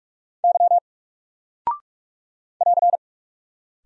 Morse code, heard as a keyed tone, about 44 wpm, 700 Hz. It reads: Y <AR>
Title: Single Character (Letter:Number:Punctuation:Prosign) 45wpm